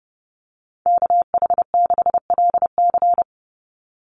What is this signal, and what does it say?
Morse code, heard as a keyed tone, about 29 wpm, 700 Hz. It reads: KH6LC
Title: Call Signs Repeated 3x - 30wpm